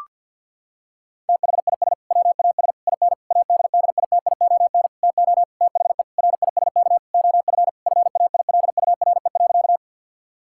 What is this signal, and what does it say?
Morse code, heard as a keyed tone, about 50 wpm, 700 Hz. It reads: THIS WAS IN ADDITION TO THE RISK OF FAILURE.
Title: Sentences from Top 1000 Words - Review Part 6 - 50wpm